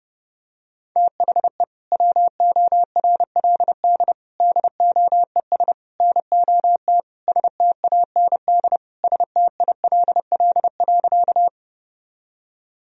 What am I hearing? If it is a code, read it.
Morse code, heard as a keyed tone, about 30 wpm, 700 Hz. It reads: THE WORLD DOES NOT STAND STILL.
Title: Sentences from Top 1000 Words - Review Part 1 - 30wpm